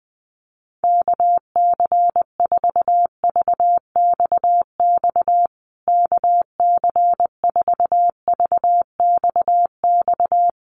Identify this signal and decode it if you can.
Morse code, heard as a keyed tone, about 20 wpm, 700 Hz. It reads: KC4VXX KC4VXX
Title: USA Call Signs Repeated 2x 20wpm